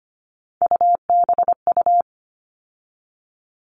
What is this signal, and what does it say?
Morse code, heard as a keyed tone, about 25 wpm, 700 Hz. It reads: UBU
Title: All Cut Numbers - 3-Digit 25wpm